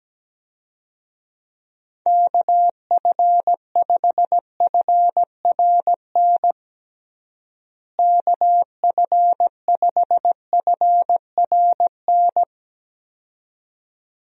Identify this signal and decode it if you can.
Morse code, heard as a keyed tone, about 17 wpm, 700 Hz. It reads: KF5FRN KF5FRN
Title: USA Call Signs - 6 Characters - Repeated 2x - 17wpm